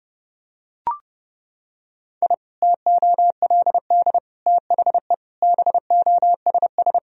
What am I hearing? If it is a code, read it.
Morse code, heard as a keyed tone, about 30 wpm, 700 Hz. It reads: I TOLD THE BOSS
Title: Sentences from Top 1000 Words - Review Part 4 - 30wpm